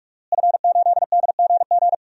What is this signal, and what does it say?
Morse code, heard as a keyed tone, about 46 wpm, 700 Hz. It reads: F8DGG
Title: DXCC-100 Call Signs - 5 Characters - Repeated 3x - 45wpm